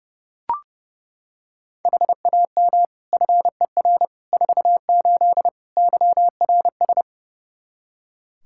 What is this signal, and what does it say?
Morse code, heard as a keyed tone, about 30 wpm, 700 Hz. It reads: HAM FER 48 YRS
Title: Practice QSO - Exchange 3 - More Info - 30wpm